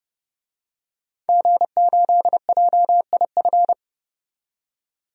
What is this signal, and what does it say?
Morse code, heard as a keyed tone, about 30 wpm, 700 Hz. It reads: G8JIF
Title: DXCC-100 Call Signs - 5 Characters - 30wpm